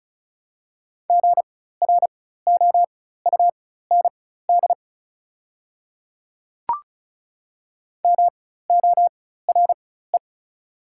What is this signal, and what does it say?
Morse code, heard as a keyed tone, about 36 wpm, spaced as for 16 wpm, 690 Hz. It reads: GROUND MORE
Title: 4x Character Spacing - Top 500 Words - 35wpm